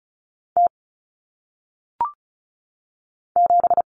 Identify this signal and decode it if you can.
Morse code, heard as a keyed tone, about 36 wpm, 700 Hz. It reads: T 7
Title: Common Cut Numbers - 1-Digit 35wpm